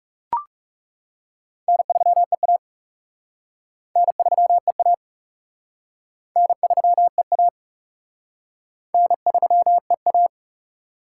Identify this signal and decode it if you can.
Morse code, heard as a keyed tone, about 37 wpm, 700 Hz. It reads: N3EA N3EA N3EA N3EA
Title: Speed Racing - USA 1x2 Call Signs 30wpm